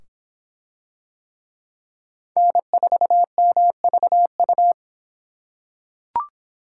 Morse code, heard as a keyed tone, about 23 wpm, 700 Hz. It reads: N4MVU